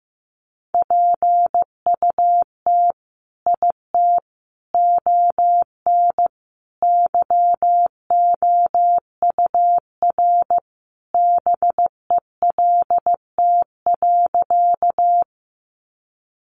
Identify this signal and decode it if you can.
Morse code, heard as a keyed tone, about 14 wpm, 700 Hz. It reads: PUT IT ON YOUR BELT.
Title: Sentences from Top 1400 Words 15wpm